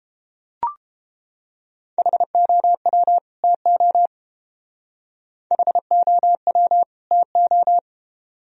Morse code, heard as a keyed tone, about 28 wpm, 700 Hz. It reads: HOW TO HOW TO
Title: Speed-Racing - Sets of 2 Words 22wpm